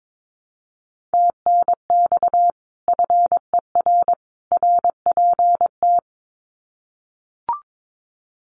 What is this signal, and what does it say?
Morse code, heard as a keyed tone, about 22 wpm, 700 Hz. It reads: TNX FER RPT
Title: Common QSO Phrases 22wpm